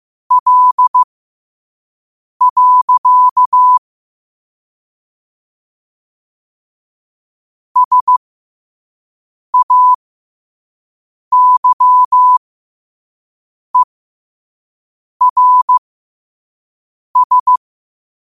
Morse code, heard as a keyed tone, about 15 wpm, spaced as for 5 wpm, 1000 Hz. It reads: L. SAYERS
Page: qotd-at-5-wpm